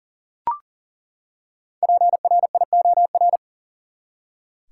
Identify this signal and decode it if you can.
Morse code, heard as a keyed tone, about 40 wpm, 700 Hz. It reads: PRIOR
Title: Five Letter Words - 40wpm